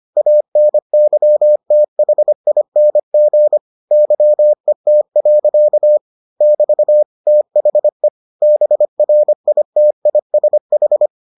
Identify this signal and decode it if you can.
Morse code, heard as a keyed tone, about 25 wpm, 600 Hz. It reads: ANYTHING YET. <BT> THE BRITISH